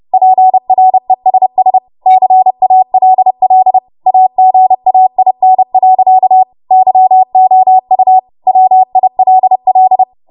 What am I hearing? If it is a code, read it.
Morse code, heard as a keyed tone, about 30 wpm, 750 Hz. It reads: PRESS CALL AGAIN. YOU WILL
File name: Morse / 30WPM